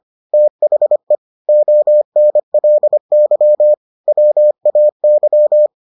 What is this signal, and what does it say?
Morse code, heard as a keyed tone, about 25 wpm, 600 Hz. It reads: THE ONLY WAY